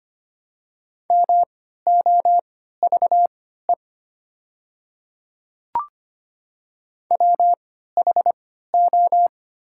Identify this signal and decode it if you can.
Morse code, heard as a keyed tone, about 25 wpm, spaced as for 14 wpm, 700 Hz. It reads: MOVE WHO
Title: 3x Character Spacing - Top 200 Words - 25wpm